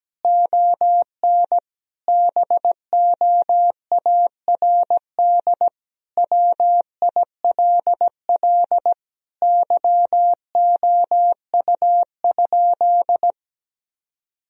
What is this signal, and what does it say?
Morse code, heard as a keyed tone, about 17 wpm, 700 Hz. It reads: ON BOARD WILL YOU?
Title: Sentences from Top 1000 Words - Review Part 4 - 17wpm